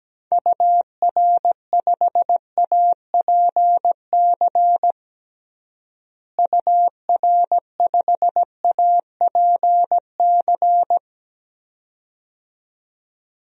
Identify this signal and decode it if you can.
Morse code, heard as a keyed tone, about 17 wpm, 700 Hz. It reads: UR5APC UR5APC
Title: DXCC-100 Call Signs - 6 Characters - Repeated 2x - 17wpm